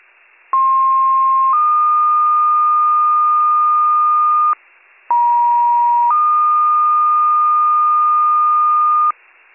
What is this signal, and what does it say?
Two-tone page, heard as a one-second tone, then a three-second tone, about 1060 then 1220 Hz, about 950 then 1220 Hz. It is heard from 0.5 to 9.1 s.